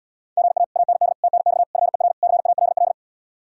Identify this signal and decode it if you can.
Morse code, heard as a keyed tone, about 77 wpm, 700 Hz. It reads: GA AND TNX FER CALL